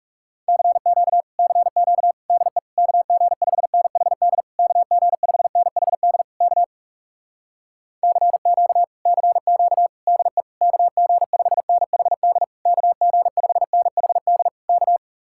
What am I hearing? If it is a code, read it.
Morse code, heard as a keyed tone, about 40 wpm, 700 Hz. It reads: CQ CQ DE KG5NHD KG5NHD K CQ CQ DE KG5NHD KG5NHD K